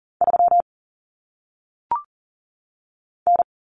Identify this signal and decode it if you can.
Morse code, heard as a keyed tone, about 40 wpm, 700 Hz. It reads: 3 N
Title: Common Cut Numbers - 1-Digit 40wpm